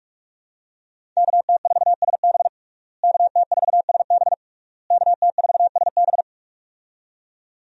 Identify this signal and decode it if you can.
Morse code, heard as a keyed tone, about 41 wpm, 700 Hz. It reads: KT4SB KT4SB KT4SB
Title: USA Call Signs - 5 Characters - Repeated 3x - 45wpm